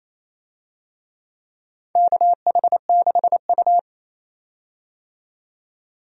Morse code, heard as a keyed tone, about 28 wpm, 700 Hz. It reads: KH6U
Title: DXCC-100 Call Signs - 4 Characters - 28wpm